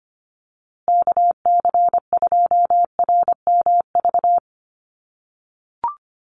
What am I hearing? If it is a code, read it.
Morse code, heard as a keyed tone, about 25 wpm, 700 Hz. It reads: KC2RMV
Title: USA Call Signs Repeated 3x 25wpm